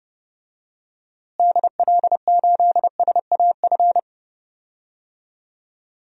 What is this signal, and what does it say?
Morse code, heard as a keyed tone, about 30 wpm, 700 Hz. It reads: DL8SAF
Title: DXCC-100 Call Signs - 6 Characters - Repeated 3x - 30wpm